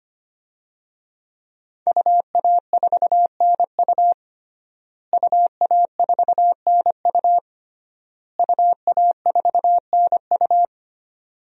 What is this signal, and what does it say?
Morse code, heard as a keyed tone, about 25 wpm, 700 Hz. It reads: UA4NU UA4NU UA4NU
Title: DXCC-100 Call Signs - 5 Characters - Repeated 3x - 25wpm